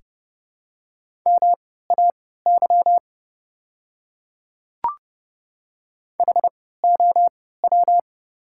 Morse code, heard as a keyed tone, about 30 wpm, spaced as for 17 wpm, 700 Hz. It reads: MAY HOW
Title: 3x Character Spacing - Top 200 Words - 30wpm